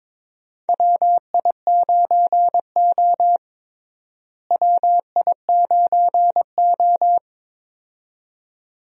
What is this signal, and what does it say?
Morse code, heard as a keyed tone, about 22 wpm, 700 Hz. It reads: WI9O WI9O
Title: USA Call Signs - 4 Characters - Repeated 2x - 22wpm